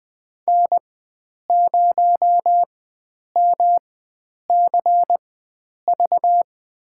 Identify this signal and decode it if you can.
Morse code, heard as a keyed tone, about 20 wpm, spaced as for 9 wpm, 700 Hz. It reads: N0MCV
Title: USA Call Signs - 5 Characters - 4x Character Spacing - 20wpm